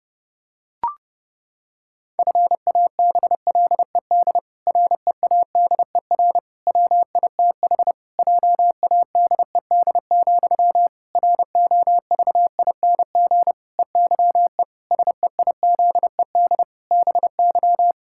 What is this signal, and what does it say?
Morse code, heard as a keyed tone, about 30 wpm, 700 Hz. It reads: FABLED READER WITH JADED, ROVING EYE SEIZED BY
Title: Pangram Challenge v2 - 30wpm